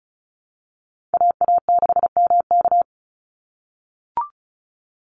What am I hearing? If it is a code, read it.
Morse code, heard as a keyed tone, about 34 wpm, 700 Hz. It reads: AA6MK
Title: Call Signs - 35wpm